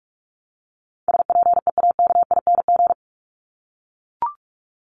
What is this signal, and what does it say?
Morse code, heard as a keyed tone, about 44 wpm, 700 Hz. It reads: SPEAKING
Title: ING Suffix - Top 100 Words with Base Words 45wpm